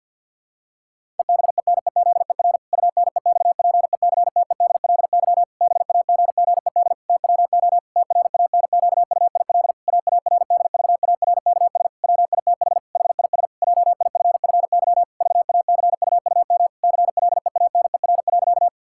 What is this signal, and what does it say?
Morse code, heard as a keyed tone, about 52 wpm, 700 Hz. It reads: EBENEZER UNEXPECTEDLY BAGGED TWO TRANQUIL AARDVARKS WITH HIS JIFFY VACUUM CLEANER.
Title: Pangram Challenge v2 - 50wpm